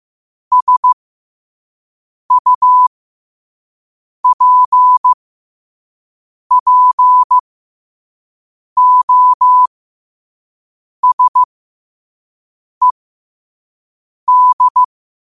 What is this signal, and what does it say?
Morse code, heard as a keyed tone, about 15 wpm, spaced as for 5 wpm, 1000 Hz. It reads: SUPPOSED
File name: qotdfiles / 5wpm